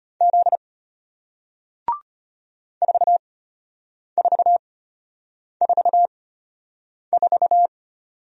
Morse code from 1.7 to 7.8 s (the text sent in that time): 4444